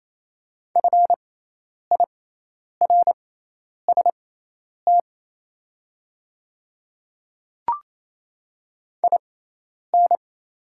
Morse code, heard as a keyed tone, about 27 wpm, spaced as for 10 wpm, 700 Hz. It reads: FIRST IN
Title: 6x Character Spacing - Top 100 Words - 28wpm